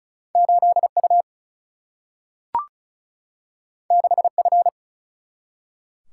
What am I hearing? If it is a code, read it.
Morse code, heard as a keyed tone, about 36 wpm, 700 Hz. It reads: 8U 6F
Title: Single-Letter Single-Number 35wpm